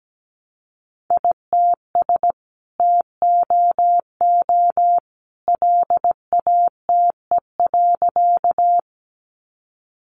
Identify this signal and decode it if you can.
Morse code, heard as a keyed tone, about 17 wpm, 700 Hz. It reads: ITS TOO LATE.
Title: Sentences from Top 600 Words 17wpm